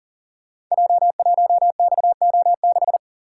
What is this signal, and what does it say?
Morse code, heard as a keyed tone, about 38 wpm, 700 Hz. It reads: J1XO6
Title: Code Groups - 5 Letter:Numbers - 40wpm